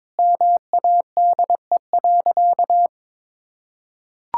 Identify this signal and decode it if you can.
Morse code, heard as a keyed tone, about 22 wpm, 700 Hz. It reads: MADE.
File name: practice / 22wpm / Sentences with 4-Word Phrases - 22wpm